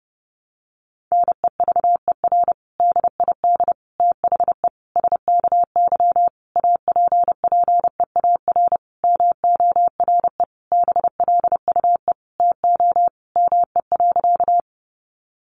Morse code, heard as a keyed tone, about 30 wpm, 700 Hz. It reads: NEVER DID THE SKY APPEAR MORE BLUE TO ME.
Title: Sentences from Top 1200 Words 30wpm